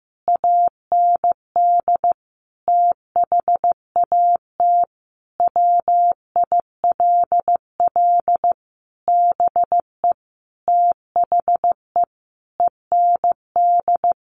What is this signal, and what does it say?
Morse code, heard as a keyed tone, about 15 wpm, 700 Hz. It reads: AND THAT WILL BE THE END
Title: Sentences from Top 1200 Words 15wpm